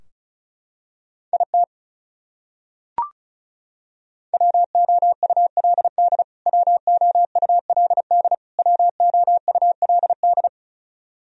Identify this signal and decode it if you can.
Morse code, heard as a keyed tone, about 34 wpm, 700 Hz. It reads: IT WOULD WOULD WOULD
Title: Top 100 Words Repeated 35wpm